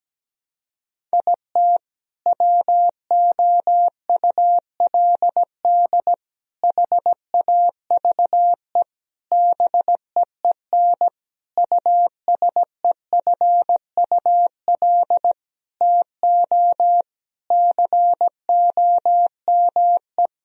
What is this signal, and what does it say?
Morse code, heard as a keyed tone, about 17 wpm, 700 Hz. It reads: IT WOULD HAVE BEEN USEFUL TO COME
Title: Sentences from Top 1000 Words - Review Part 4 - 17wpm